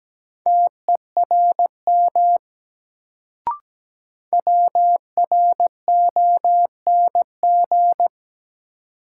Morse code, heard as a keyed tone, about 17 wpm, 700 Hz. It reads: TERM WRONG
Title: Sentences from Top 900 Words - New Word Drill - 17wpm